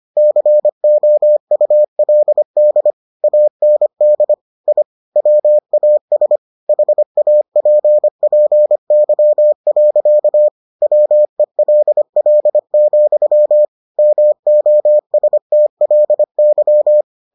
Morse code, heard as a keyed tone, about 25 wpm, 600 Hz. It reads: COULD AND I WAS HAPPY. WELL, MOSTLY